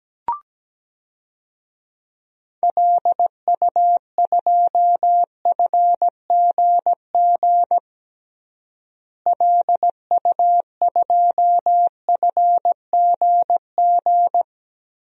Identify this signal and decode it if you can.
Morse code, heard as a keyed tone, about 17 wpm, 700 Hz. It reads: LU2FGG LU2FGG
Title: DXCC-100 Call Signs - 6 Characters - Repeated 2x - 17wpm